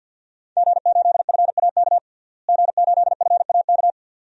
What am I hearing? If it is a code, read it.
Morse code, heard as a keyed tone, about 52 wpm, 700 Hz. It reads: G8FAK G8FAK
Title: DXCC-100 Call Signs - 5 Characters - Repeated 2x - 50wpm